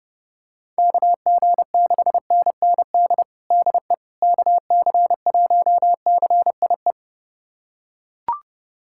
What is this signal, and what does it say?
Morse code, heard as a keyed tone, about 30 wpm, 700 Hz. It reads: KG6NND DE KC1CIE
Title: USA Callsign DE Callsign - 6 Characters - 30wpm